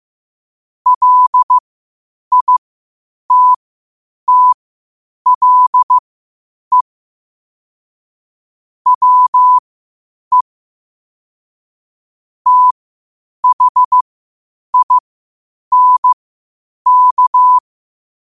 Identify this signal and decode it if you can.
Morse code, heard as a keyed tone, about 15 wpm, spaced as for 8 wpm, 1000 Hz. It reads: LITTLE WE THINK
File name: qotdfiles / 8wpm